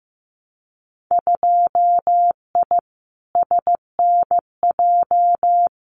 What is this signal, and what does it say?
Morse code, heard as a keyed tone, about 15 wpm, 700 Hz. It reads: 2I SNJ
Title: Winter Field Day Exchanges Encore 15wpm